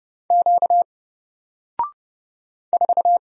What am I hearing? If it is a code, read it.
Morse code, heard as a keyed tone, about 31 wpm, 700 Hz. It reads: Q 4
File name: Single-Letter-Number-30wpm